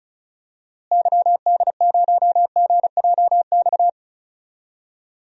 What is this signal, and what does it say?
Morse code, heard as a keyed tone, about 34 wpm, 700 Hz. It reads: YD0GJX
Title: DXCC-100 Call Signs - 6 Characters - Repeated 2x - 35wpm